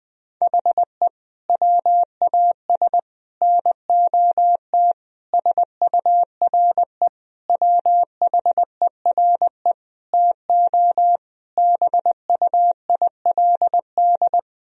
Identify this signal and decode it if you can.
Morse code, heard as a keyed tone, about 21 wpm, 700 Hz. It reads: HE WAS NOT SURE WHERE TO BUILD
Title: Sentences from Top 1000 Words - Review Part 1 - 20wpm